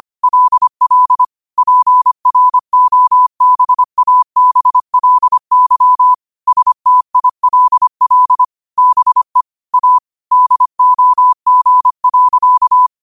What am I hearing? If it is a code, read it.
Morse code, heard as a keyed tone, about 25 wpm, 1000 Hz. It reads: LL PROBABLY STILL BE A DOG.
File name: qotdfiles / 25wpm